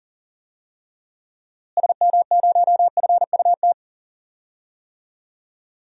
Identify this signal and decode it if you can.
Morse code, heard as a keyed tone, about 40 wpm, 700 Hz. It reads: SM0FUT
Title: DXCC-100 Call Signs - 6 Characters - Repeated 2x - 40wpm